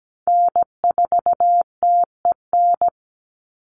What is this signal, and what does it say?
Morse code, heard as a keyed tone, about 17 wpm, 700 Hz. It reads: N4TEN
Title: USA Call Signs 17wpm